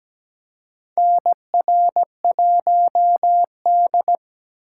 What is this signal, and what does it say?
Morse code, heard as a keyed tone, about 17 wpm, 700 Hz. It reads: NR1D
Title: USA Call Signs - 4 Characters - 4x Character Spacing - 17wpm